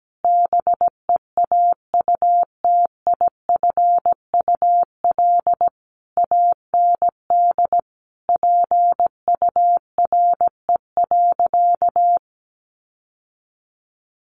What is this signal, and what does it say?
Morse code, heard as a keyed tone, about 17 wpm, 700 Hz. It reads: BEAUTIFUL AND PURE.
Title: Sentences from Top 1500 Words 17wpm